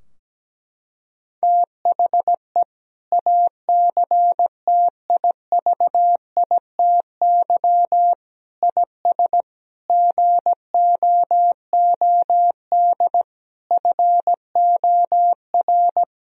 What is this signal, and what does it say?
Morse code, heard as a keyed tone, about 17 wpm, 700 Hz. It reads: THE ACTIVITY IS GOOD FOR
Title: Sentences from Top 1000 Words - Review Part 4 - 17wpm